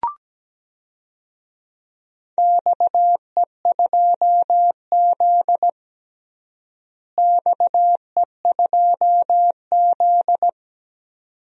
Morse code, heard as a keyed tone, about 17 wpm, 700 Hz. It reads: XE2Z XE2Z
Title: DXCC-100 Call Signs - 4 Characters - Repeated 2x - 17wpm